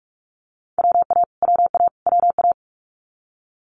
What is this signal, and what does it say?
Morse code, heard as a keyed tone, about 46 wpm, 690 Hz. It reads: WA WA WA
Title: Multi-letter Phonemes Repeated 3x 45wpm